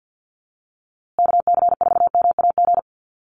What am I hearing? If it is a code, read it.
Morse code, heard as a keyed tone, about 48 wpm, 700 Hz. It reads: KC4MAG
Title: USA Call Signs Repeated 3x 50wpm